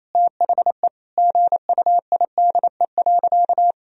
Morse code, heard as a keyed tone, about 28 wpm, 700 Hz. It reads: THE GUIDE.
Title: Sentences from Top 1000 Words - Review Part 3 - 28wpm